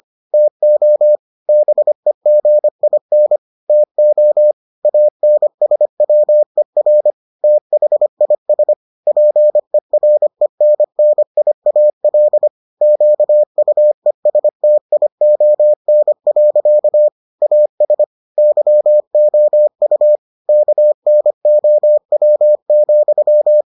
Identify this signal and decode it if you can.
Morse code, heard as a keyed tone, about 25 wpm, 600 Hz. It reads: TO BEGIN TO ANSWER THIS PERENNIAL QUESTION. AS YOU KNOW,